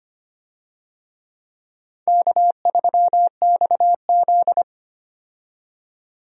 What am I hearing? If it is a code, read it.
Morse code, heard as a keyed tone, about 25 wpm, 700 Hz. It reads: K3XZ
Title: USA Call Signs - 4 Characters - 25wpm